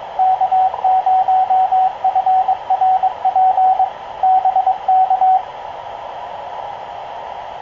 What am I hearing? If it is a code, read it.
Morse code, heard as a keyed tone, about 23 wpm, 750 Hz, with radio noise. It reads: K0FRP BK
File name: k0frp ki0ii wa8bxn fox utc